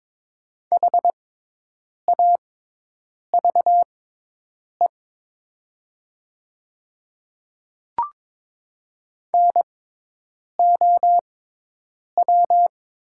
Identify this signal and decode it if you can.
Morse code, heard as a keyed tone, about 22 wpm, spaced as for 8 wpm, 700 Hz. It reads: HAVE NOW
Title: 6x Character Spacing - Top 200 Words - 22wpm